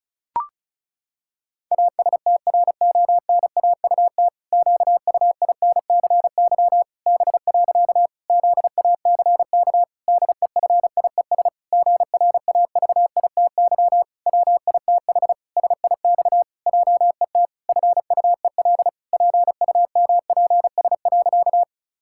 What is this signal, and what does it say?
Morse code, heard as a keyed tone, about 34 wpm, 700 Hz. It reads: ASTRONAUT QUINCY B. ZACK DEFIES GRAVITY WITH SIX JET FUEL PUMPS.
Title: Pangram Challenge v2 - 35wpm